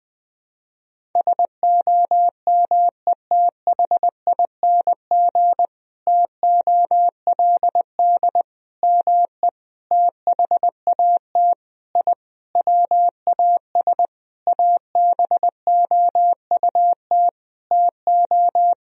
Morse code, heard as a keyed tone, about 20 wpm, 700 Hz. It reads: SOMETHING TOLD ME THAT I WAS ABOUT TO
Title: Sentences from Top 1000 Words - Review Part 1 - 20wpm